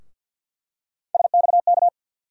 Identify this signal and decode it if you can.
Morse code, heard as a keyed tone, about 47 wpm, 690 Hz. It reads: SXK
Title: USA Call Sign Postfixes - 3 Letters 50wpm